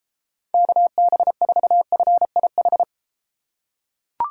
Morse code, heard as a keyed tone, about 32 wpm, 700 Hz. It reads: KB4FIH